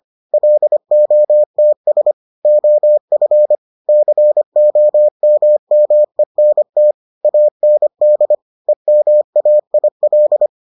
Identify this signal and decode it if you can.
Morse code, heard as a keyed tone, about 25 wpm, 600 Hz. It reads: LOTS OF COMMENT AND EMAIL